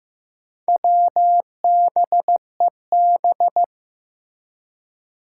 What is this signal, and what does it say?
Morse code, heard as a keyed tone, about 15 wpm, 700 Hz. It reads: WBEB
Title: Four Challenging Letter-Number Combinations 15wpm